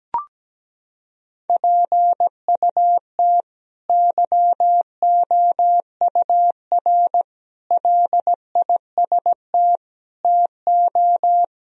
Morse code, heard as a keyed tone, about 17 wpm, 700 Hz. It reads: PUT YOUR LIST TO
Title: Sentences from Top 1000 Words - Review Part 2 - 17wpm